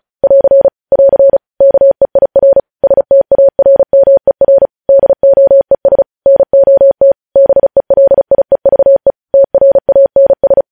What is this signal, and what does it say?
Morse code, heard as a keyed tone, about 36 wpm, 550 Hz. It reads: <AR> <AR> KEIR STARMER DOES NOT BELIEVE TRANS
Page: news headlines in morse code at 35 wpm